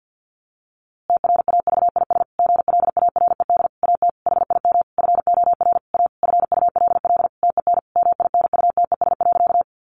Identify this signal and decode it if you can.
Morse code, heard as a keyed tone, about 50 wpm, 700 Hz. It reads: TRAVIS GLARED AT HIM FOR A FULL TEN MINUTES.
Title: A Sound Of Thunder 50wpm